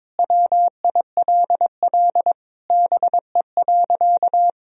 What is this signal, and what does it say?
Morse code, heard as a keyed tone, about 22 wpm, 700 Hz. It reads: WILL BE.
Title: Sentences from Top 1000 Words - Review Part 4 - 22wpm